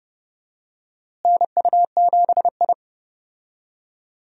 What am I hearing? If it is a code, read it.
Morse code, heard as a keyed tone, about 30 wpm, 700 Hz. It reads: NU7I
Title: USA Call Signs - 4 Characters - Repeated 2x - 30wpm